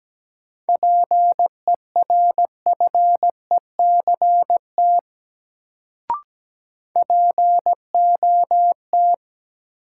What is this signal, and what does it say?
Morse code, heard as a keyed tone, about 17 wpm, 700 Hz. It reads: PERFECT POT
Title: Sentences from Top 1000 Words - New Word Drill - 17wpm